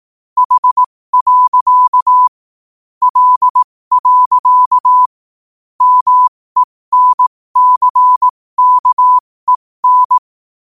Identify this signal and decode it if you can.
Morse code, heard as a keyed tone, about 18 wpm, spaced as for 15 wpm, 1000 Hz. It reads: H. L. MENCKEN
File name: qotdfiles / 15wpm